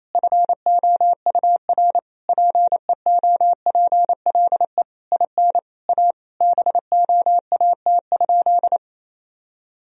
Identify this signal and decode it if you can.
Morse code, heard as a keyed tone, about 28 wpm, 700 Hz. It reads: FOUR PEOPLE IN A BOAT?
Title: Sentences from Top 1000 Words - Review Part 3 - 28wpm